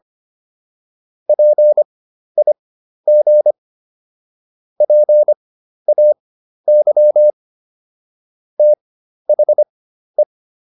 Morse code, heard as a keyed tone, about 25 wpm, spaced as for 12 wpm, 600 Hz. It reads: PIG PAY THE